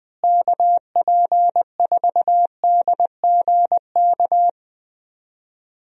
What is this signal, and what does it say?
Morse code, heard as a keyed tone, about 21 wpm, 700 Hz. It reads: KP4DGK
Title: USA Call Signs - 6 Characters - Repeated 2x - 20wpm